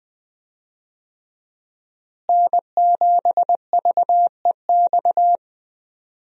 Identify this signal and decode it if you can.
Morse code, heard as a keyed tone, about 20 wpm, 700 Hz. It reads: N7VEX
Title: USA Call Signs - 5 Characters - Repeated 2x - 20wpm